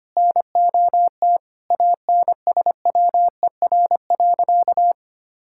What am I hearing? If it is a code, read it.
Morse code, heard as a keyed tone, about 25 wpm, 700 Hz. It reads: NOT ANSWER.
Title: Sentences from Top 1000 Words - Review Part 6 - 25wpm